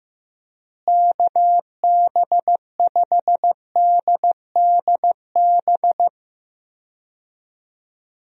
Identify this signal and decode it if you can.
Morse code, heard as a keyed tone, about 16 wpm, 700 Hz. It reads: KB5DDB
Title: USA Call Signs - 6 Characters - Repeated 2x - 15wpm